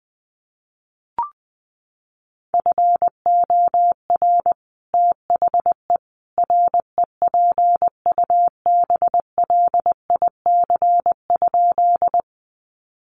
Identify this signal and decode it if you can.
Morse code, heard as a keyed tone, about 20 wpm, 700 Hz. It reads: FOR THE REPUBLIC?
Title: Sentences from Top 1500 Words 20wpm